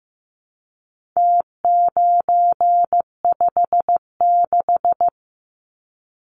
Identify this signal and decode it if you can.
Morse code, heard as a keyed tone, about 15 wpm, 700 Hz. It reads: T956
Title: Numbers - Contest Style - 4-Digit 15wpm